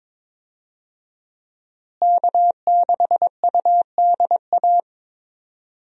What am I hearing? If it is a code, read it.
Morse code, heard as a keyed tone, about 22 wpm, 700 Hz. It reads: K6UDA
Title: USA Call Signs - 5 Characters - Repeated 2x - 22wpm